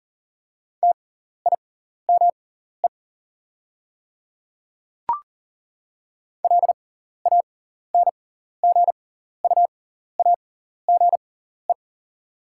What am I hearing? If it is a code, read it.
Morse code, heard as a keyed tone, about 40 wpm, spaced as for 14 wpm, 690 Hz. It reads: TIME LANGUAGE
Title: 6x Character Spacing - Top 500 Words - 40wpm